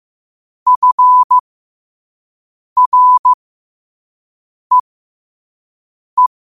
Morse code, heard as a keyed tone, about 15 wpm, spaced as for 5 wpm, 1000 Hz. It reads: FREE
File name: 5wpm